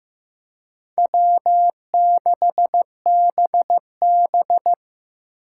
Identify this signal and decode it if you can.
Morse code, heard as a keyed tone, about 18 wpm, 700 Hz. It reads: W6BB
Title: Speed Racing - USA 1x2 Call Signs 15wpm